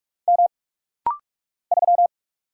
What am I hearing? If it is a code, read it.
Morse code, heard as a keyed tone, about 44 wpm, 700 Hz. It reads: M 3
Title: Single Letter-Number - Rapid-Fire 45wpm